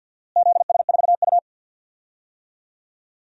Morse code, heard as a keyed tone, about 51 wpm, 700 Hz. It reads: ZS4U
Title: DXCC-100 Call Signs - 4 Characters - Repeated 2x - 50wpm